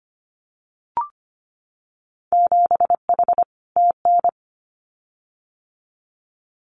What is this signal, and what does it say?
Morse code, heard as a keyed tone, about 25 wpm, 700 Hz. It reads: 7H TN
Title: Winter Field Day Exchanges Encore 25wpm